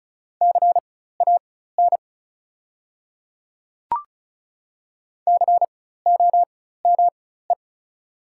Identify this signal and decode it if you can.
Morse code, heard as a keyed tone, about 36 wpm, spaced as for 16 wpm, 700 Hz. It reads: CAN COME
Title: 4x Character Spacing - Top 100 Words - 35wpm